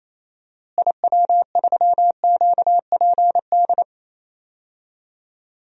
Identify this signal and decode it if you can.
Morse code, heard as a keyed tone, about 28 wpm, 700 Hz. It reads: IW3QPD
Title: DXCC-100 Call Signs - 6 Characters - Repeated 3x - 28wpm